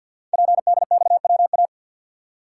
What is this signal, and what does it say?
Morse code, heard as a keyed tone, about 51 wpm, 700 Hz. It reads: PDXWA
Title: Code Groups - 5 Letter:Numbers - 50wpm